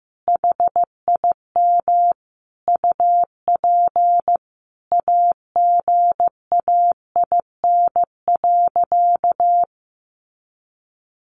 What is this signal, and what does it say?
Morse code, heard as a keyed tone, about 15 wpm, 700 Hz. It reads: HIM UP AGAIN.
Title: Sentences from Top 1400 Words 15wpm